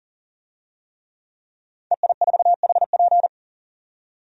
Morse code, heard as a keyed tone, about 38 wpm, 700 Hz. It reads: EI4HP
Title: DXCC-100 Call Signs - 5 Characters - 40wpm